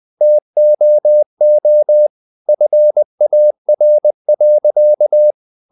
Morse code, heard as a keyed tone, about 20 wpm, 600 Hz. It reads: TOO FAR.
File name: the old man and the sea / the old man and the sea part 1 20wpm